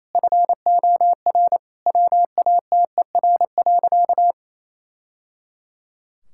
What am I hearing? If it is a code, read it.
Morse code, heard as a keyed tone, about 28 wpm, 700 Hz. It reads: FOR WATER.
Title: Sentences from Top 1000 Words - Review Part 5 - 28wpm